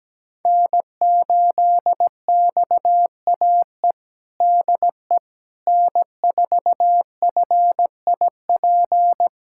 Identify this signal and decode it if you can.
Morse code, heard as a keyed tone, about 17 wpm, 700 Hz. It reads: N8XAE DE N4FIP
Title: USA Callsign DE Callsign - 5 Characters - 17wpm